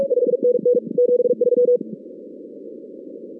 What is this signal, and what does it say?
Morse code, heard as a keyed tone, about 42 wpm, 500 Hz. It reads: 5NN 73
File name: 2016 arrl dx cw / 2016-ARRL-DX-CW-K5TR- - 0-17